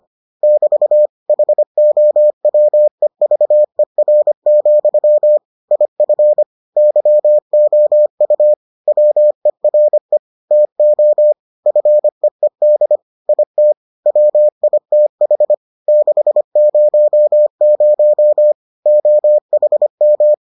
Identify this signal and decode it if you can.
Morse code, heard as a keyed tone, about 25 wpm, 600 Hz. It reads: <BT> HOWEVER, IF YOU WERE TO FEED IT WITH 600 OHM